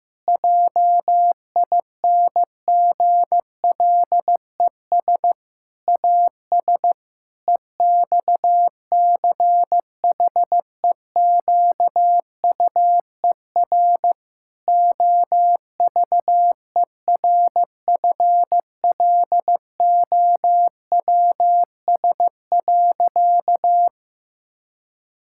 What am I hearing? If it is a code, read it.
Morse code, heard as a keyed tone, about 15 wpm, 700 Hz. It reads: JINGLES AS EXCHEQUER OVERFLOWS.
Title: Pangram Challenge v2 - 15wpm